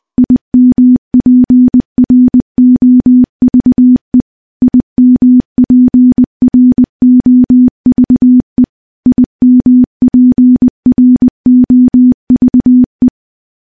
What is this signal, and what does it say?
Morse code, heard as a keyed tone, about 20 wpm, 250 Hz. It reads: IMPROVE IMPROVE IMPROVE
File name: morse-training-20WPM